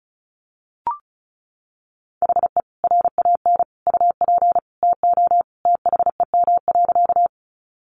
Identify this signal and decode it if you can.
Morse code, heard as a keyed tone, about 34 wpm, 700 Hz. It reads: HE RAN UP TO THEM.
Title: Sentences from Top 500 Words 35wpm